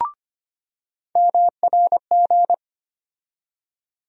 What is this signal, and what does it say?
Morse code, heard as a keyed tone, about 25 wpm, 700 Hz. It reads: MRG
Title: USA Call Sign Postfixes - 3 Letters 25wpm